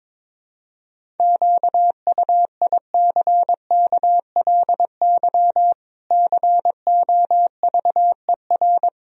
Morse code, heard as a keyed tone, about 22 wpm, 700 Hz. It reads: QUICKLY COVER
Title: Sentences from Top 1000 Words - Review Part 1 - 22wpm